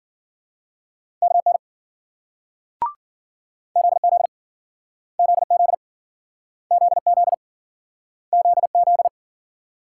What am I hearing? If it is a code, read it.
Morse code, heard as a keyed tone, about 54 wpm, 700 Hz. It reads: KN ZZ ZZ ZZ ZZ